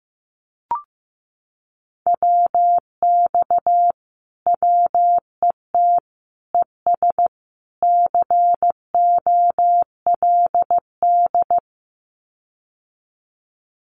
Morse code, heard as a keyed tone, about 15 wpm, 700 Hz. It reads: WX WET ES COLD
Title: Common QSO Phrases Encore 15wpm